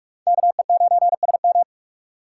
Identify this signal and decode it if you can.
Morse code, heard as a keyed tone, about 57 wpm, 700 Hz. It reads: KE9SM